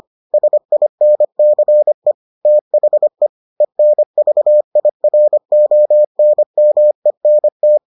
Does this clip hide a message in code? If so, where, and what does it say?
Yes: Morse code from 0.3 to 7.8 s, heard as a keyed tone, about 25 wpm, 600 Hz. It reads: SINCE THE ENVIRONMENT